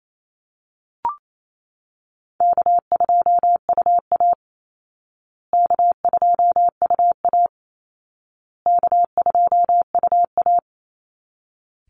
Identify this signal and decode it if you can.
Morse code, heard as a keyed tone, about 28 wpm, 700 Hz. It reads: K2UA K2UA K2UA
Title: Easy Call Signs Repeated 3x - 28wpm